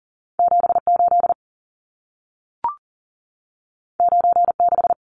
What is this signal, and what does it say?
Morse code, heard as a keyed tone, about 40 wpm, 700 Hz. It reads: 78 96
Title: Numbers - 2-Digit 40wpm